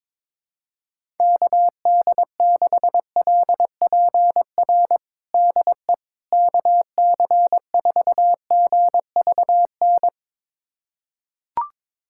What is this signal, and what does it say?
Morse code, heard as a keyed tone, about 22 wpm, 700 Hz. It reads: KD6LPR DE KC4GVN
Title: USA Callsign DE Callsign - 6 Characters - 22wpm